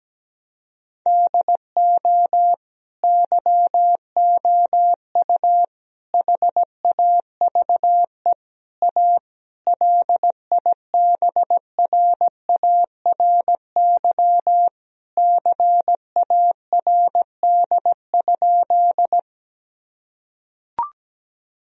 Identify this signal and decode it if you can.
Morse code, heard as a keyed tone, about 17 wpm, 700 Hz. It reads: DO YOU HAVE A LIBRARY CARD?
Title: Sentences from Top 1000 Words - Review Part 4 - 17wpm